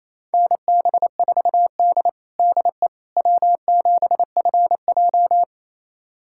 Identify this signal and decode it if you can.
Morse code, heard as a keyed tone, about 28 wpm, 700 Hz. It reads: NB4D DE W7FJ
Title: USA Callsign DE Callsign - 4 Characters - 28wpm